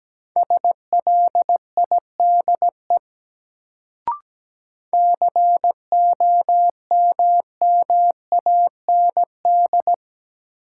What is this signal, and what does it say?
Morse code, heard as a keyed tone, about 17 wpm, 700 Hz. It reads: SLIDE COMMAND
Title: Sentences from Top 1600 Words - New Word Drill - 17wpm